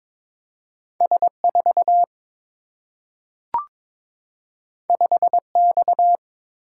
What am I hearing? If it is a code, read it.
Morse code, heard as a keyed tone, about 22 wpm, 700 Hz. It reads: S 4 5 X
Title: Single-Letter Single-Number 22wpm